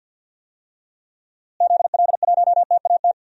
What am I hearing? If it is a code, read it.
Morse code, heard as a keyed tone, about 47 wpm, 700 Hz. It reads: ZL1TAT
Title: DXCC-100 Call Signs - 6 Characters - Repeated 3x - 50wpm